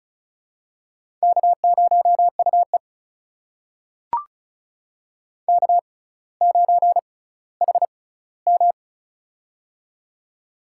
Morse code, heard as a keyed tone, about 36 wpm, spaced as for 18 wpm, 700 Hz. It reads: K0UE K9HM
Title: USA Call Signs - 4 Characters - 6x Character Spacing - 35wpm